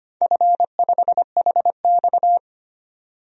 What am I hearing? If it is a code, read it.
Morse code, heard as a keyed tone, about 25 wpm, 700 Hz. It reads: F5HX
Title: DXCC-100 Call Signs - 4 Characters - 25wpm